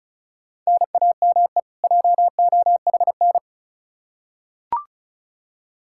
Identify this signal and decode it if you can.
Morse code, heard as a keyed tone, about 36 wpm, 700 Hz. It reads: NAME JOHN
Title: Practice QSO - Exchange 1 - Beginning - 35wpm